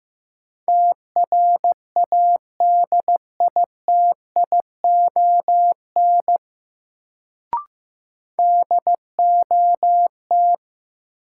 Morse code, heard as a keyed tone, about 15 wpm, 700 Hz. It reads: TRADITION DOT
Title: Sentences from Top 1500 Words - New Word Drill Alt - 15wpm